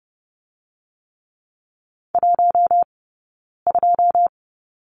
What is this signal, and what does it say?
Morse code, heard as a keyed tone, about 30 wpm, 700 Hz. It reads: 1 2
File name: Lesson 17d - Letter u - Two Characters 30wpm